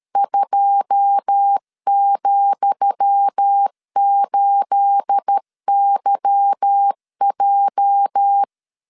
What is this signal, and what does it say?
Morse code, heard as a keyed tone, about 13 wpm, 800 Hz. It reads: 2,8YJ
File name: LEZIONI / 10 SIGNS / 10D COMA 12wpm 2544